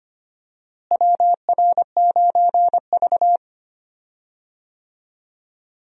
Morse code, heard as a keyed tone, about 25 wpm, 700 Hz. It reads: WR9V
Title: USA Call Signs - 4 Characters - Repeated 3x - 25wpm